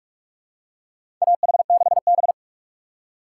Morse code, heard as a keyed tone, about 46 wpm, 700 Hz. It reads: AH6B
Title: USA Call Signs - 4x Character Spacing - 45wpm